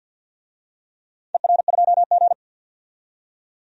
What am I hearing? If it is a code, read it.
Morse code, heard as a keyed tone, about 53 wpm, 700 Hz. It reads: ER2G
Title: DXCC-100 Call Signs - 4 Characters - 50wpm